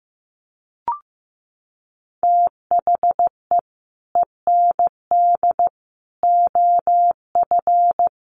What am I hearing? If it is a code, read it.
Morse code, heard as a keyed tone, about 15 wpm, 700 Hz. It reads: THE END OF
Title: Sets of 5 Words Encore 15wpm